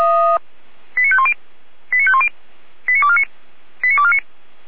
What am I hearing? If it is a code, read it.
Five-tone selective call (ZVEI), heard as quick runs of five tones, tones of about 70 ms. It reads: 89510 89510 89249 89249